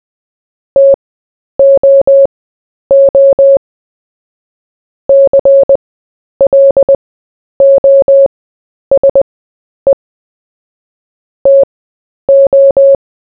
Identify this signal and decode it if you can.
Morse code, heard as a keyed tone, about 20 wpm, spaced as for 10 wpm, 550 Hz. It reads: TOO CLOSE TO
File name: cw / fists / Short-Story-4-10wpm